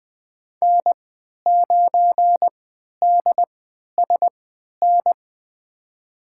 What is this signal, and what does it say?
Morse code, heard as a keyed tone, about 20 wpm, spaced as for 11 wpm, 700 Hz. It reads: N9DSN
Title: USA Call Signs - 5 Characters - 3x Character Spacing - 20wpm